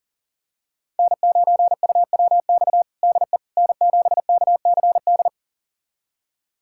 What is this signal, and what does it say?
Morse code, heard as a keyed tone, about 37 wpm, 700 Hz. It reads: N9UWX DE N7KCD
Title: USA Callsign DE Callsign - 5 Characters - 40wpm